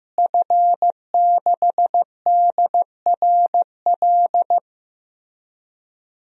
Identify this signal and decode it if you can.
Morse code, heard as a keyed tone, about 15 wpm, 700 Hz. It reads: F6DRL
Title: DXCC-100 Call Signs - 5 Characters - Repeated 3x - 15wpm